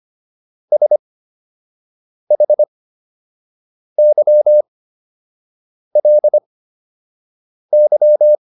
Morse code, heard as a keyed tone, about 25 wpm, spaced as for 6 wpm, 600 Hz. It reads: SHYLY